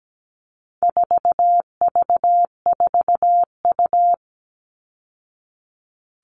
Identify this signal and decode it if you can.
Morse code, heard as a keyed tone, about 17 wpm, 700 Hz. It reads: 4V4U
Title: All Cut Numbers - 4-Digit 17wpm